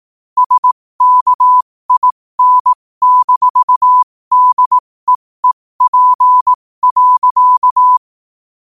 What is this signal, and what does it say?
Morse code, heard as a keyed tone, about 18 wpm, spaced as for 15 wpm, 1000 Hz. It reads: SKIN-DEEP.
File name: qotdfiles / 15wpm